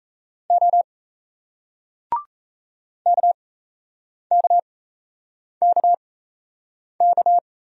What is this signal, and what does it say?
Morse code, heard as a keyed tone, about 33 wpm, spaced as for 8 wpm, 700 Hz. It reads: O KKKK
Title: Speed-Racing Single Letters 28wpm